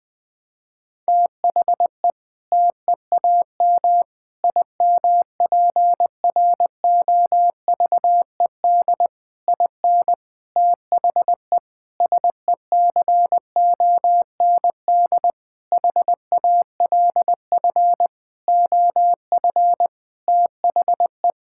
Morse code, heard as a keyed tone, about 20 wpm, 700 Hz. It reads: THE TEAM IMPROVED IN THE SECOND HALF OF THE
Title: Sentences with 7-Word Phrases - 20wpm